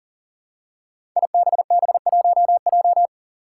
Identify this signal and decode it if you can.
Morse code, heard as a keyed tone, about 40 wpm, 700 Hz. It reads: IBB1J
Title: Five Challenging Letter-Number Combinations 40wpm